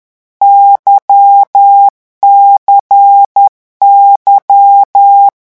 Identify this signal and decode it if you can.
Morse code, heard as a keyed tone, about 10 wpm, 800 Hz. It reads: YCY
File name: morse mp3 / lecon2-vitesse 10